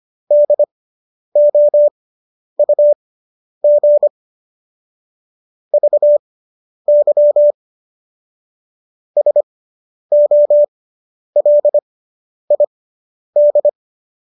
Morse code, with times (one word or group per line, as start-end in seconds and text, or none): 0.3-4.1 DOUG
5.7-7.5 VY
9.2-13.7 SOLID